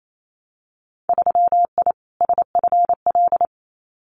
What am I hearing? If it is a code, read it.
Morse code, heard as a keyed tone, about 28 wpm, 700 Hz. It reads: 3I SFL
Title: Winter Field Day Exchanges 28wpm